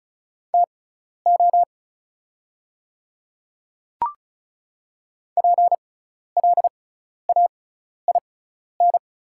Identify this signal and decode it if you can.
Morse code, heard as a keyed tone, about 34 wpm, spaced as for 12 wpm, 690 Hz. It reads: TO PLAIN
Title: 6x Character Spacing - Top 500 Words - 35wpm